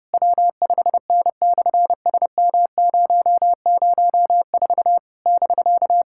Morse code, heard as a keyed tone, about 30 wpm, 700 Hz. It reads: W5N/SM004 <BK>
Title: SOTA QSO Practice 30wpm